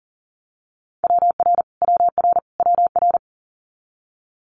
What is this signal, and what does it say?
Morse code, heard as a keyed tone, about 40 wpm, 700 Hz. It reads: WR WR WR
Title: Multi-letter Phonemes Repeated 3x 40wpm